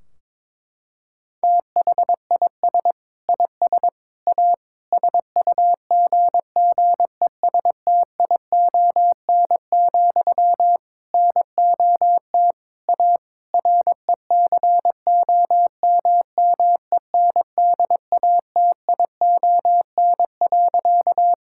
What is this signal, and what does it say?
Morse code, heard as a keyed tone, about 22 wpm, 700 Hz. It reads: THIS IS A SUGGESTION, NOT A RECOMMENDATION.
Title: Sentences from Top 1600 Words 22wpm